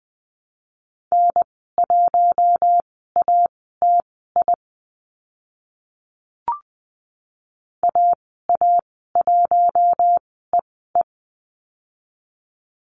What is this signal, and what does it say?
Morse code, heard as a keyed tone, about 20 wpm, spaced as for 14 wpm, 700 Hz. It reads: N1ATI AA1EE
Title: 042 Lesson 8g - Number 1 - Call Signs 20wpm